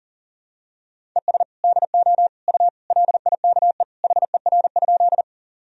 Morse code, heard as a keyed tone, about 40 wpm, 700 Hz. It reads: ES DO U LIKE HER?
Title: Sentences from Top 100 Words and Abbreviations 40wpm